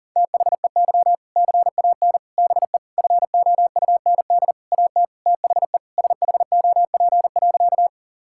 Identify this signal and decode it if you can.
Morse code, heard as a keyed tone, about 40 wpm, 700 Hz. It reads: THEY CAN BE FOUND AT THE SHOP.
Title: Sentences from Top 1600 Words 40wpm